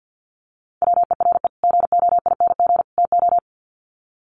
Morse code, heard as a keyed tone, about 50 wpm, 700 Hz. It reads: WERE GOING TO
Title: Sets of 3 Words 50wpm